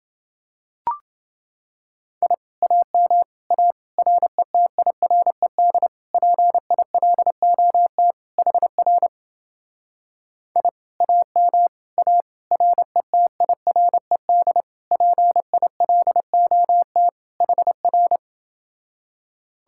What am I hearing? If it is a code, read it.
Morse code, heard as a keyed tone, about 30 wpm, 700 Hz. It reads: I AM A RETIRED PILOT HR I AM A RETIRED PILOT HR